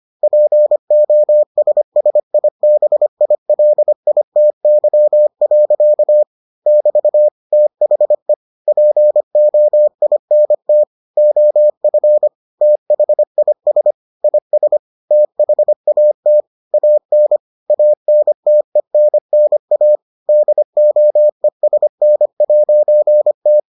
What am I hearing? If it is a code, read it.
Morse code, heard as a keyed tone, about 25 wpm, 600 Hz. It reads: POSSIBILITY. <BT> THE POINT OF THIS IS THAT AN ANTENNA DOESN'T